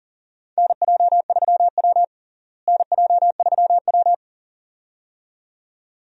Morse code, heard as a keyed tone, about 40 wpm, 700 Hz. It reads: NJ3W NJ3W
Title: USA Call Signs - 4 Characters - Repeated 2x - 40wpm